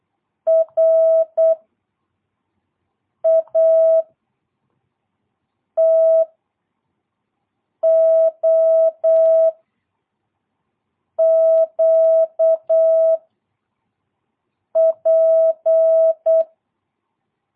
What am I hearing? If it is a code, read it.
Morse code, heard as a keyed tone, about 8 wpm, 660 Hz. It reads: R A T O Q P
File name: jh8cbh / 2022cw